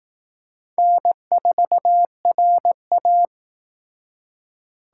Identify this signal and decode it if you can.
Morse code, heard as a keyed tone, about 18 wpm, 700 Hz. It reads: N4RA
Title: Speed Racing - USA 1x2 Call Signs 15wpm